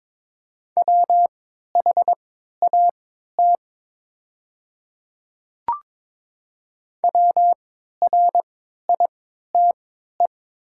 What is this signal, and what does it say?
Morse code, heard as a keyed tone, about 21 wpm, spaced as for 12 wpm, 700 Hz. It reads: WHAT WRITE
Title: 3x Character Spacing - Top 100 Words - 22wpm